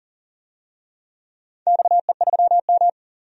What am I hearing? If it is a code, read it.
Morse code, heard as a keyed tone, about 40 wpm, 700 Hz. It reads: XE3M
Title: DXCC-100 Call Signs - 4 Characters - 40wpm